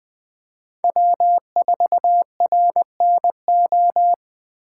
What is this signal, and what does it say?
Morse code, heard as a keyed tone, about 20 wpm, 700 Hz. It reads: W4RNO
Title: 2023 POTA Hunter Call Signs - 20wpm